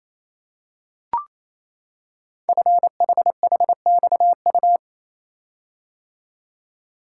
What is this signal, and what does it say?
Morse code, heard as a keyed tone, about 27 wpm, 700 Hz. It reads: FHHXU
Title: Code Groups - 5 Letters - 28wpm